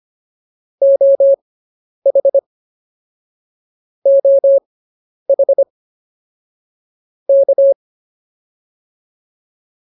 Morse code, heard as a keyed tone, about 25 wpm, spaced as for 10 wpm, 550 Hz. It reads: OH OH K